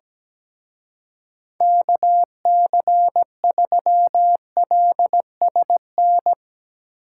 Morse code, heard as a keyed tone, about 20 wpm, 700 Hz. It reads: KC3LSN
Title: Speed Racing - USA 2x3 Call Signs 17wpm